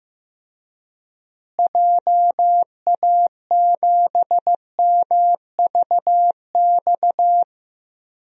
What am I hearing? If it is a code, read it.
Morse code, heard as a keyed tone, about 15 wpm, 700 Hz. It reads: JA7MVX
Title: DXCC-100 Call Signs - 6 Characters - Repeated 3x - 15wpm